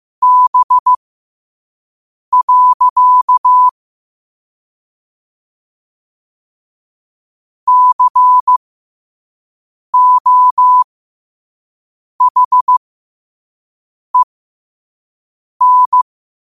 Morse code, heard as a keyed tone, about 15 wpm, spaced as for 5 wpm, 1000 Hz. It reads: B. COHEN
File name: qotdfiles / 5wpm